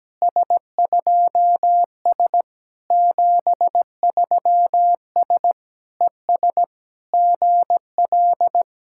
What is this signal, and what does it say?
Morse code, heard as a keyed tone, about 17 wpm, 700 Hz. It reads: S2S 73S ES GL
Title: SOTA QSO Practice 17wpm